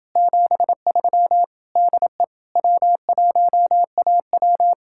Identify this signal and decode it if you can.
Morse code, heard as a keyed tone, about 29 wpm, 700 Hz. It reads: 73 DE W1AW